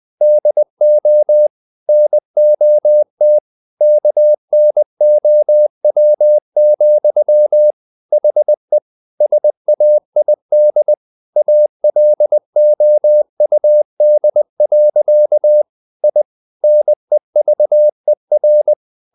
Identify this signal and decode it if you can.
Morse code, heard as a keyed tone, about 20 wpm, 600 Hz. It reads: DO NOT KNOW, HE SAID ALOUD. I NEVER
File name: the old man and the sea / the old man and the sea part 1 20wpm